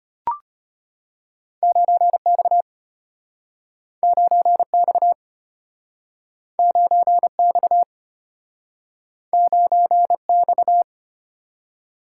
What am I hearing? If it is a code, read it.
Morse code, heard as a keyed tone, about 33 wpm, 700 Hz. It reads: 9X 9X 9X 9X